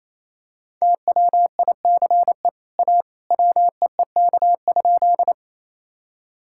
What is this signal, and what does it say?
Morse code, heard as a keyed tone, about 28 wpm, 700 Hz. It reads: TWICE A WEEK?
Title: Sentences from Top 1600 Words 28wpm